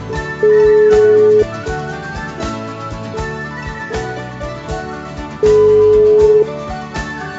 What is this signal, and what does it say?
Ringback tone (Europe, 425 Hz).